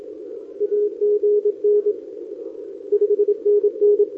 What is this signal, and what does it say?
Morse code, heard as a keyed tone, about 27 wpm, 410 Hz. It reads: AGN 5NN
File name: OK QRP 2026 / OKQRP2026